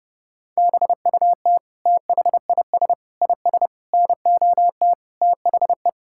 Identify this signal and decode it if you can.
Morse code, heard as a keyed tone, about 30 wpm, 700 Hz. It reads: BUT THIS IS NOT THE
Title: Sentences from Top 1600 Words 30wpm